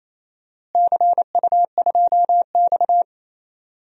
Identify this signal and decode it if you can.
Morse code, heard as a keyed tone, about 27 wpm, 700 Hz. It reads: CU2X
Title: DXCC-100 Call Signs - 4 Characters - Repeated 3x - 28wpm